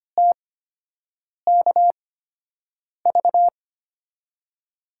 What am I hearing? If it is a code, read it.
Morse code, heard as a keyed tone, about 25 wpm, spaced as for 7 wpm, 700 Hz. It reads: TKV
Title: USA Call Signs - 5 Characters - 8x Character Spacing - 25wpm